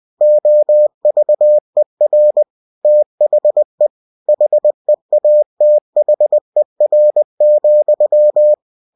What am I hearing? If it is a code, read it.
Morse code, heard as a keyed tone, about 20 wpm, 600 Hz. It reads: OVER THE HEATHER,
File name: cw-books / winnie-the-pooh / winnie-20wpm-page-0034